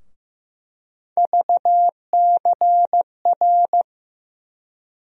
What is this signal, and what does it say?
Morse code, heard as a keyed tone, about 15 wpm, 700 Hz. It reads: VCR
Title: USA Call Sign Postfixes - 3 Letters 15wpm